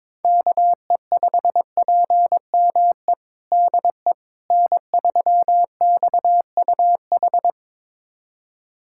Morse code, heard as a keyed tone, about 22 wpm, 700 Hz. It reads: KE5PME DE N3XUH
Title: USA Callsign DE Callsign 22wpm